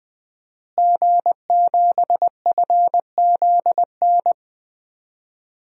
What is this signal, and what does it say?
Morse code, heard as a keyed tone, about 20 wpm, 700 Hz. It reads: G7FZN
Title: DXCC-100 Call Signs - 5 Characters - Repeated 2x - 20wpm